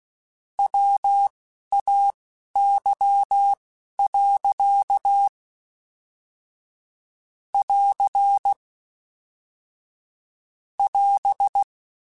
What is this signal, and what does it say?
Morse code, heard as a keyed tone, about 16 wpm, spaced as for 11 wpm, 800 Hz. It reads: WAY. <AR> <AS>